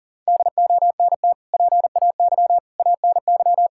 Morse code, heard as a keyed tone, about 40 wpm, 700 Hz. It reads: DONT PAY ANY
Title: Sentences from Top 1000 Words - Review Part 6 - 40wpm